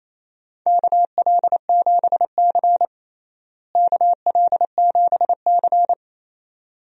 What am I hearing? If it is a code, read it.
Morse code, heard as a keyed tone, about 28 wpm, 700 Hz. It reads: KL7C KL7C
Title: DXCC-100 Call Signs - 4 Characters - Repeated 2x - 28wpm